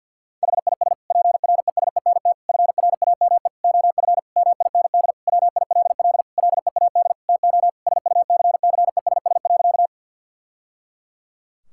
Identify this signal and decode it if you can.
Morse code, heard as a keyed tone, about 52 wpm, 700 Hz. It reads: HIS PRESENT FRAME OF MIND WILL LEAD TO SUCCESS.
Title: Sentences from Top 1000 Words - Review Part 6 - 50wpm